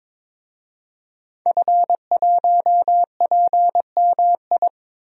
Morse code, heard as a keyed tone, about 22 wpm, 700 Hz. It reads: F1PMI
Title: DXCC-100 Call Signs - 5 Characters - 22wpm